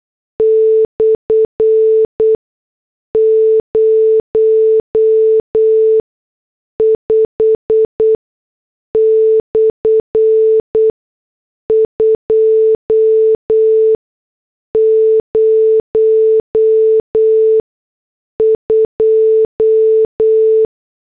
Morse code, heard as a keyed tone, about 8 wpm, spaced as for 6 wpm, 440 Hz. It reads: /05/202
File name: morse / headlines